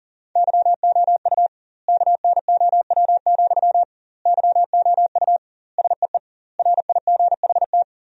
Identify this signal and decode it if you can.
Morse code, heard as a keyed tone, about 37 wpm, 700 Hz. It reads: YOU KNOW, YOU SEE RIGHT
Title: 2x Word-Spacing - Sentences from Top 200 Words - 40wpm